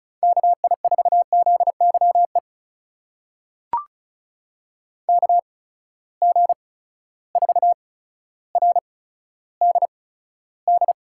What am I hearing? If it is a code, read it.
Morse code, heard as a keyed tone, about 36 wpm, spaced as for 15 wpm, 700 Hz. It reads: KI4ZYE KG4RDD
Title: USA Call Signs - 6 Characters - 8x Character Spacing - 35wpm